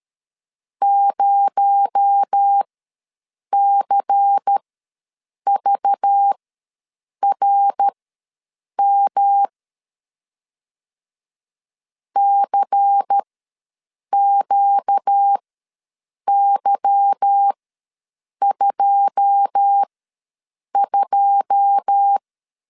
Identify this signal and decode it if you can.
Morse code, heard as a keyed tone, about 13 wpm, spaced as for 7 wpm, 800 Hz. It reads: 0CVRM CQY22
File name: LEZIONI / 9 2 ws 12wpm 3406